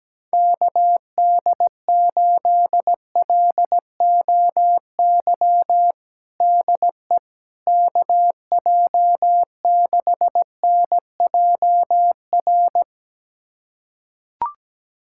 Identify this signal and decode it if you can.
Morse code, heard as a keyed tone, about 17 wpm, 700 Hz. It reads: KD8LOY DE KJ6NJR
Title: USA Callsign DE Callsign - 6 Characters - 17wpm